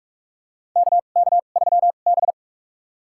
Morse code, heard as a keyed tone, about 46 wpm, spaced as for 32 wpm, 700 Hz. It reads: KK3B
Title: USA Call Signs - 4 Characters - 2x Character Spacing - 45wpm